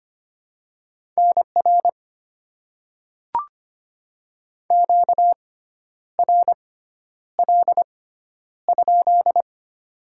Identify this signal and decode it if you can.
Morse code, heard as a keyed tone, about 25 wpm, spaced as for 10 wpm, 700 Hz. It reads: NR QRL?
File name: Top 90 QSO Elements - 6x Character Spacing - 25wpm